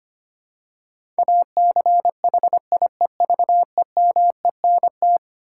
Morse code, heard as a keyed tone, about 25 wpm, 700 Hz. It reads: ACHIEVEMENT
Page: morse-code-ninja-practice-25wpm